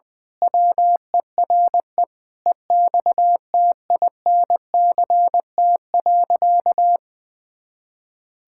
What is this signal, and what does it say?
Morse code, heard as a keyed tone, about 20 wpm, 700 Hz. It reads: WERE EXTINCT.
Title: Pangram Challenge v2 - 20wpm